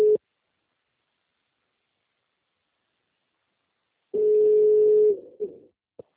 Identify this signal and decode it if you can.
Ringback tone (Europe, 425 Hz).